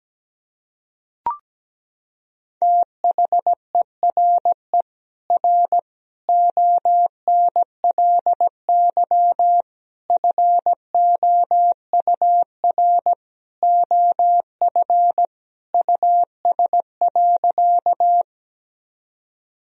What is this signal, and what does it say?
Morse code, heard as a keyed tone, about 17 wpm, 700 Hz. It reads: THERE R ONLY FOUR OF US.
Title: Sentences from Top 300 Words and Abbreviations 17wpm